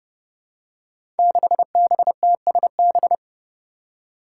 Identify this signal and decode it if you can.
Morse code, heard as a keyed tone, about 30 wpm, 700 Hz. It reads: 6BTSB
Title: Five Challenging Letter-Number Combinations 30wpm